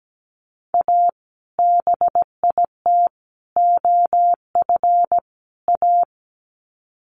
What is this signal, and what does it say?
Morse code, heard as a keyed tone, about 17 wpm, 700 Hz. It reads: A BIT OF A
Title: Sets of 4 Words 17wpm